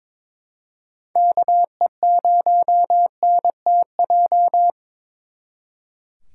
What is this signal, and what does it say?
Morse code, heard as a keyed tone, about 26 wpm, 700 Hz. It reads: KE0NTJ
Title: Speed Racing - USA 2x3 Call Signs 22wpm